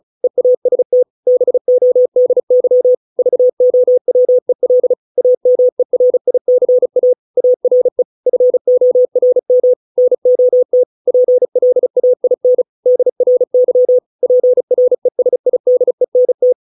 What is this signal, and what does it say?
Morse code, heard as a keyed tone, about 34 wpm, 500 Hz. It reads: EAST BODY VOWEL AMERICA ARE FORM NOT PLAIN DRY PRESIDENT